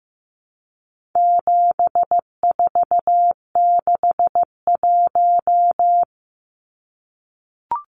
Morse code, heard as a keyed tone, about 15 wpm, 700 Hz. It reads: 7461
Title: Numbers - 4-Digit 15wpm